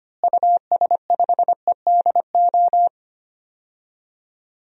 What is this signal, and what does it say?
Morse code, heard as a keyed tone, about 25 wpm, 700 Hz. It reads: US5EDO
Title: DXCC-100 Call Signs - 6 Characters - Repeated 2x - 25wpm